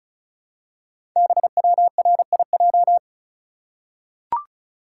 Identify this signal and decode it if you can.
Morse code, heard as a keyed tone, about 33 wpm, 690 Hz. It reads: BWRIJ
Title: Code Groups - 5 Letters - 35wpm